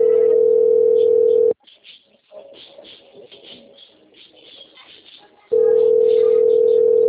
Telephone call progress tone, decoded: ringback tone

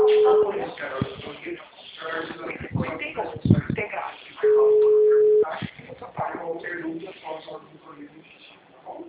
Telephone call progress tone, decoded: ringback tone

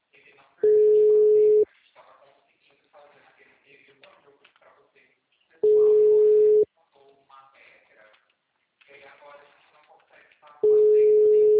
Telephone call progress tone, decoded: ringback tone

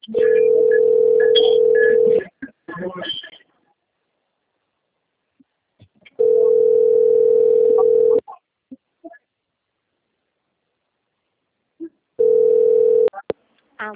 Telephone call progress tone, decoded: ringback tone

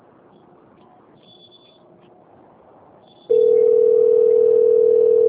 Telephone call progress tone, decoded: ringback tone

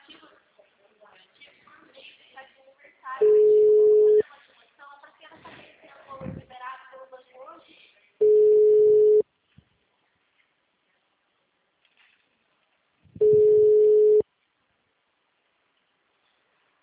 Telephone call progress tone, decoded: ringback tone

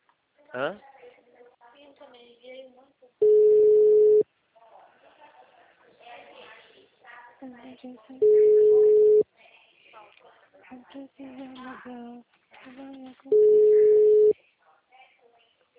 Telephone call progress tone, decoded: ringback tone

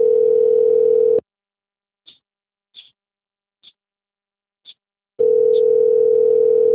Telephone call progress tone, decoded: ringback tone

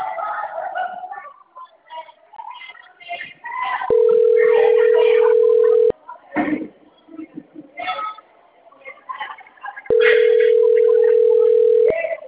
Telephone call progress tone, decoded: ringback tone

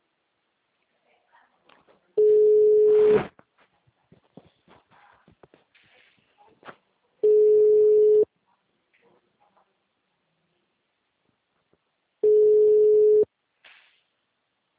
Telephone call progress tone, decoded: ringback tone

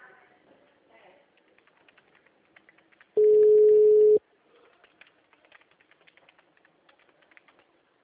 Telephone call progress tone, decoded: ringback tone

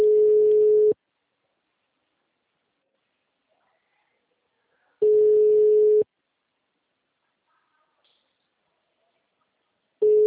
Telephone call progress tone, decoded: ringback tone

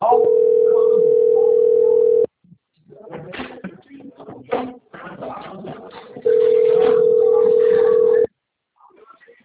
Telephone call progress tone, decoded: ringback tone

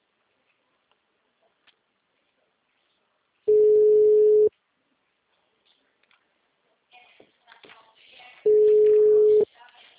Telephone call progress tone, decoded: ringback tone

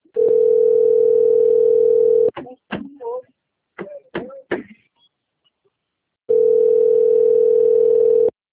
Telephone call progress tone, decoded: ringback tone

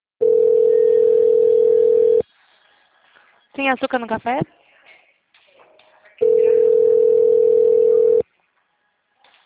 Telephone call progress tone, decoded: ringback tone

